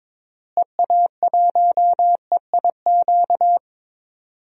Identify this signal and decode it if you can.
Morse code, heard as a keyed tone, about 22 wpm, 700 Hz. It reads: EA1EIQ
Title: DXCC-100 Call Signs - 6 Characters - Repeated 3x - 22wpm